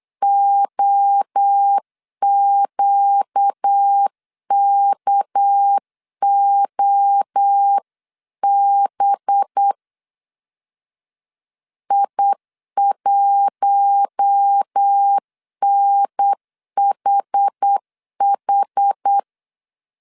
Morse code, heard as a keyed tone, about 8 wpm, 800 Hz. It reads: OQKOB I1NHH
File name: 8wpm